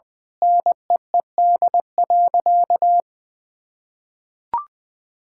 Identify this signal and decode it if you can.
Morse code, heard as a keyed tone, about 20 wpm, 700 Hz. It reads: NEED.
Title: Sentences from Top 1000 Words - Review Part 5 - 20wpm